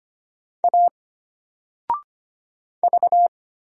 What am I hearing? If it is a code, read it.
Morse code, heard as a keyed tone, about 25 wpm, 700 Hz. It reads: A V
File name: Single-Letter-Number-25wpm